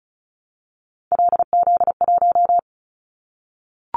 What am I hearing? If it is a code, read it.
Morse code, heard as a keyed tone, about 36 wpm, 690 Hz. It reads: LZ1
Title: Callsign Prefixes for Top 25 DXCC - 35wpm